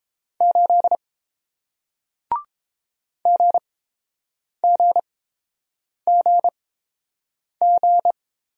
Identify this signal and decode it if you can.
Morse code, heard as a keyed tone, about 30 wpm, spaced as for 7 wpm, 700 Hz. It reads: 8 GGGG